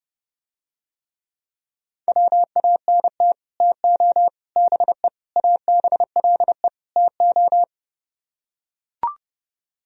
Morse code, heard as a keyed tone, about 31 wpm, 700 Hz. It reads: WANT TO BE ABLE TO